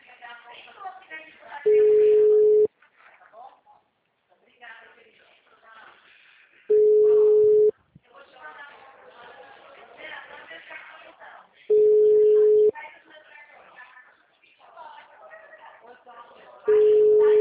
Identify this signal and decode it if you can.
Ringback tone (Europe, 425 Hz), with speech.
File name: out- -2010- - - .46222